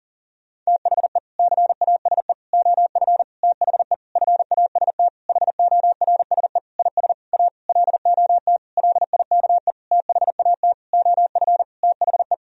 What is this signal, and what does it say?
Morse code, heard as a keyed tone, about 40 wpm, 700 Hz. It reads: THE CASE OF THE FAST HORSE IS A LOT LIKE THAT OF THE
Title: Sentences from Top 1000 Words - Review Part 2 - 40wpm